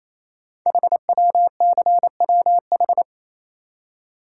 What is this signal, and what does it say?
Morse code, heard as a keyed tone, about 27 wpm, 700 Hz. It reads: HWCWH
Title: Code Groups - 5 Letters - 28wpm